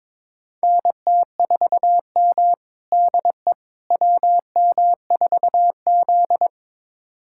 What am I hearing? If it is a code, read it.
Morse code, heard as a keyed tone, about 21 wpm, 700 Hz. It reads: NT4M DE WM4Z
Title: USA Callsign DE Callsign - 4 Characters - 22wpm